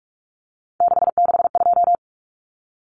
Morse code, heard as a keyed tone, about 44 wpm, 700 Hz. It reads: 662
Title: Numbers - 3-Digit 45wpm